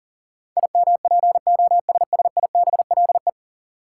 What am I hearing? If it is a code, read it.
Morse code, heard as a keyed tone, about 39 wpm, 700 Hz. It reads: IMPOSSIBLE
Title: Sentences from Top 1100 Words - New Word Drill - 40wpm